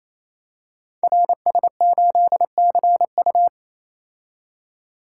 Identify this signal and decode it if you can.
Morse code, heard as a keyed tone, about 29 wpm, 700 Hz. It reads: RS8CU
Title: Code Groups - 5 Letter:Numbers - 28wpm